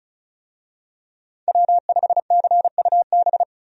Morse code, heard as a keyed tone, about 36 wpm, 700 Hz. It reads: W5CUB
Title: USA Call Signs - 5 Characters - 35wpm